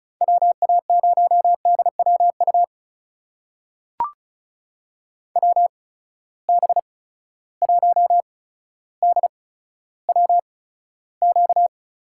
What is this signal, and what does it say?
Morse code, heard as a keyed tone, about 36 wpm, spaced as for 15 wpm, 700 Hz. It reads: WA0DWU WB1DWQ
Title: USA Call Signs - 6 Characters - 8x Character Spacing - 35wpm